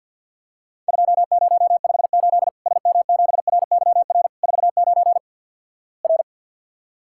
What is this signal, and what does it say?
Morse code, heard as a keyed tone, about 50 wpm, 700 Hz. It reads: 2058 SM7RYR 49 R
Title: Worked All Europe DX Contest - QTC Exchange - 50wpm